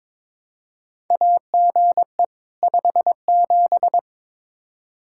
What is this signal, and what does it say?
Morse code, heard as a keyed tone, about 22 wpm, 700 Hz. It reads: AGE 57
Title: Practice QSO - Exchange 3 - More Info - 22wpm